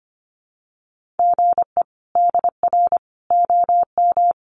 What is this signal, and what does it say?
Morse code, heard as a keyed tone, about 25 wpm, 700 Hz. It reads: GE DR OM
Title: Common QSO Phrases 25wpm